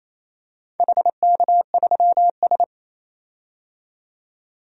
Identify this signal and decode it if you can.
Morse code, heard as a keyed tone, about 27 wpm, 690 Hz. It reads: HK3S
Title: DXCC-100 Call Signs - 4 Characters - 28wpm